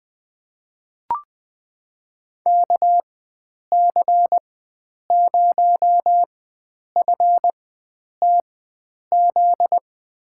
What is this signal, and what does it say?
Morse code, heard as a keyed tone, about 20 wpm, spaced as for 9 wpm, 700 Hz. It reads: KC0FTZ
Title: USA Call Signs - 6 Characters - 4x Character Spacing - 20wpm